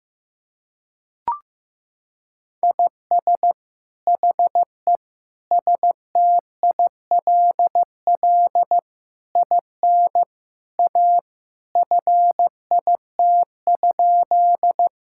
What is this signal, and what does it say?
Morse code, heard as a keyed tone, about 15 wpm, 700 Hz. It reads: IS HE STILL IN A FIT?
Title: Sentences from Top 1000 Words - Review Part 5 - 15wpm